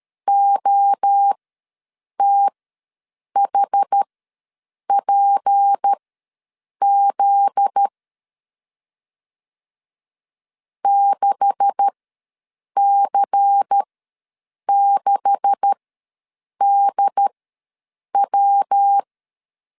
Morse code, heard as a keyed tone, about 13 wpm, spaced as for 7 wpm, 800 Hz. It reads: OTHPZ 6C6DW
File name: LEZIONI / 9 286 ws 12wpm 3351